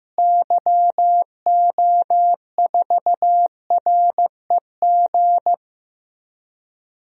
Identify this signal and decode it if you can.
Morse code, heard as a keyed tone, about 15 wpm, 700 Hz. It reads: YO4REG
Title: DXCC-100 Call Signs - 6 Characters - Repeated 3x - 15wpm